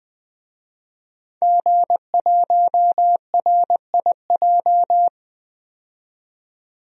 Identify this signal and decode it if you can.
Morse code, heard as a keyed tone, about 20 wpm, 700 Hz. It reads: G1RIJ
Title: DXCC-100 Call Signs - 5 Characters - 20wpm